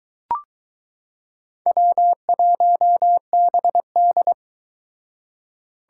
Morse code, heard as a keyed tone, about 22 wpm, 700 Hz. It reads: W1BD